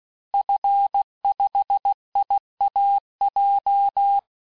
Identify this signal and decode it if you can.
Morse code, heard as a keyed tone, about 16 wpm, 800 Hz. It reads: F5IAJ
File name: cours-du-lundi files / cours-1592--teste-FIN-de-QSO-vitesse-15mpm